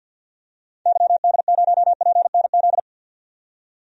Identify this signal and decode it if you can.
Morse code, heard as a keyed tone, about 51 wpm, 700 Hz. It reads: YD0PNZ
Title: DXCC-100 Call Signs - 6 Characters - 50wpm